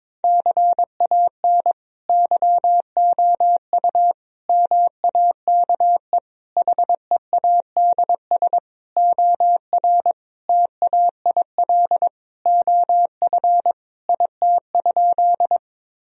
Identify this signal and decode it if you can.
Morse code, heard as a keyed tone, about 21 wpm, 700 Hz. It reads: CAN YOU MAKE HEADS OR TAIL OF IT?
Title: Sentences from Top 1000 Words - Review Part 3 - 22wpm